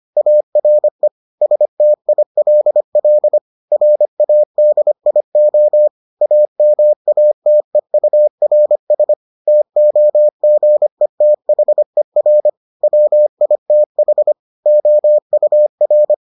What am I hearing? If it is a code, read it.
Morse code, heard as a keyed tone, about 25 wpm, 600 Hz. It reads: ARE STILL RADIO AMATEURS TOGETHER WITH OUR